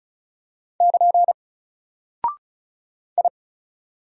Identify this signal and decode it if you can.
Morse code, heard as a keyed tone, about 36 wpm, 700 Hz. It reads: <KN> I
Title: Single Character (Letter:Number:Punctuation:Prosign) Alt 35wpm